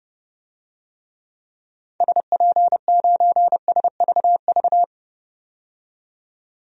Morse code, heard as a keyed tone, about 30 wpm, 700 Hz. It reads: SP9SVV
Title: DXCC-100 Call Signs - 6 Characters - 30wpm